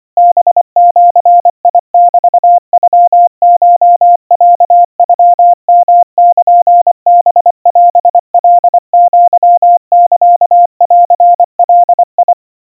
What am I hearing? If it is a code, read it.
Morse code (Japanese Wabun code), heard as a keyed tone, about 25 wpm, 700 Hz. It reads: ハジメノコロノヨルハオカアサンガ
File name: 90 cpm 700 Hz Crispy